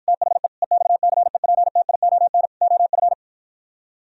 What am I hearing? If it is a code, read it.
Morse code, heard as a keyed tone, about 52 wpm, 700 Hz. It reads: THE EXCEPTION OF